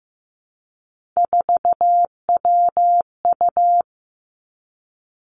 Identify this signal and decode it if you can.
Morse code, heard as a keyed tone, about 15 wpm, 700 Hz. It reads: 4WU
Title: Three Challenging Letter-Number Combinations 15wpm